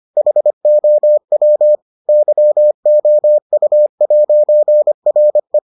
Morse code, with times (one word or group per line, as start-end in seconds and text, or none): 0.2-1.8 HOW
2.1-5.6 YOU'RE